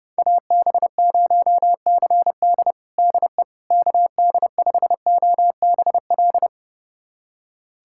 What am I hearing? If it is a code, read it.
Morse code, heard as a keyed tone, about 30 wpm, 700 Hz. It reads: AB0CD DE KD5OBL
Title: USA Callsign DE Callsign 30wpm